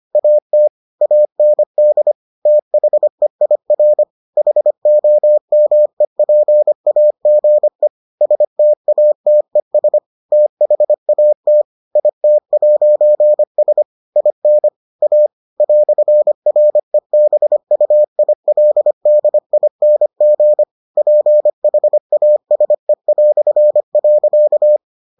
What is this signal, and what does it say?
Morse code, heard as a keyed tone, about 25 wpm, 600 Hz. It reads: AT AND THEIR HOMEPAGE STATES THAT IT'S IN A "REBUILDING PHASE".